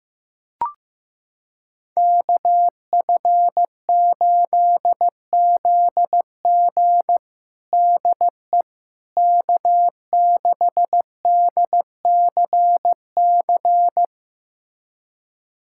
Morse code, heard as a keyed tone, about 16 wpm, 700 Hz. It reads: KF8ZG DE K6DCC
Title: USA Callsign DE Callsign - 5 Characters - 15wpm